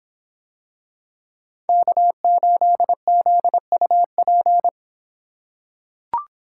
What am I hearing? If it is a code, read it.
Morse code, heard as a keyed tone, about 23 wpm, 700 Hz. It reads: K8ZUP